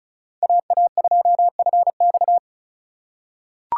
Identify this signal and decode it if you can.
Morse code, heard as a keyed tone, about 36 wpm, 700 Hz. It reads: AA2FX
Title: USA Call Signs - 5 Characters - 4x Character Spacing - 35wpm